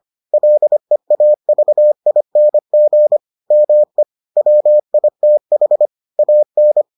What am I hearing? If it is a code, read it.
Morse code, heard as a keyed tone, about 25 wpm, 600 Hz. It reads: LEAVING ME WITH AN